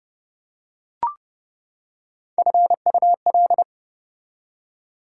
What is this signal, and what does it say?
Morse code, heard as a keyed tone, about 28 wpm, 700 Hz. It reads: FUL